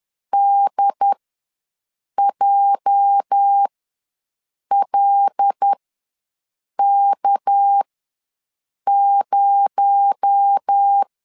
Morse code, heard as a keyed tone, about 10 wpm, spaced as for 6 wpm, 800 Hz. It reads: DJLK0
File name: LEZIONI / 8 YJ45 / 8 YJ ws 10wpm 4043